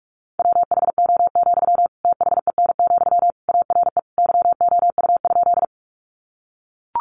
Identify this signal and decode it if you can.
Morse code, heard as a keyed tone, about 44 wpm, 700 Hz. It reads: WHO, THEN, ARE YOU?
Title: Sentences from Top 100 Words 45wpm